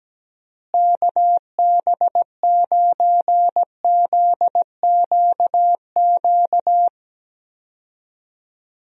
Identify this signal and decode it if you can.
Morse code, heard as a keyed tone, about 17 wpm, 700 Hz. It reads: KB9ZQQ
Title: USA Call Signs - 6 Characters - Repeated 2x - 17wpm